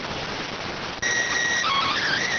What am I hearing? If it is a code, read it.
SSTV (slow-scan television), PD 290.